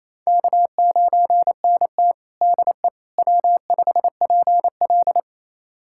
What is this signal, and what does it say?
Morse code, heard as a keyed tone, about 28 wpm, 700 Hz. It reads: K9NT DE W5PL
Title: USA Callsign DE Callsign - 4 Characters - 28wpm